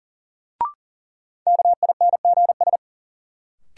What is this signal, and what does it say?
Morse code, heard as a keyed tone, about 40 wpm, 700 Hz. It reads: KINGS
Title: Sentences from Top 1000 Words - Review Part 6 - New Word Drill - 40wpm